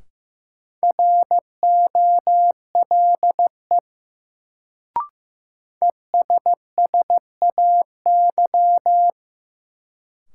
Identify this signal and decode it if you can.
Morse code, heard as a keyed tone, about 16 wpm, 700 Hz. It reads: ROLE ESSAY
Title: Sentences from Top 1500 Words - New Word Drill Alt - 15wpm